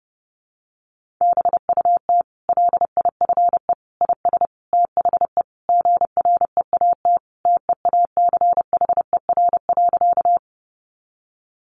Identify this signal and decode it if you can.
Morse code, heard as a keyed tone, about 30 wpm, 700 Hz. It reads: BUT LIFE IS THE GREAT TEACHER.
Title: Sentences from Top 900 Words 30wpm